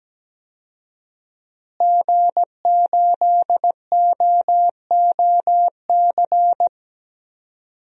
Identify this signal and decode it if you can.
Morse code, heard as a keyed tone, about 17 wpm, 700 Hz. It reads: G8OOC
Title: DXCC-100 Call Signs - 5 Characters - Repeated 3x - 17wpm